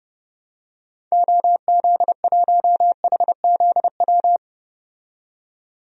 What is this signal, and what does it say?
Morse code, heard as a keyed tone, about 30 wpm, 700 Hz. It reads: OZ1HZW
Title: DXCC-100 Call Signs - 6 Characters - 30wpm